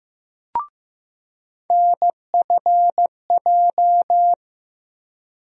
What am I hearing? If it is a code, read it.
Morse code, heard as a keyed tone, about 15 wpm, 700 Hz. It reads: NFJ
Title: USA Call Sign Postfixes - 3 Letters 15wpm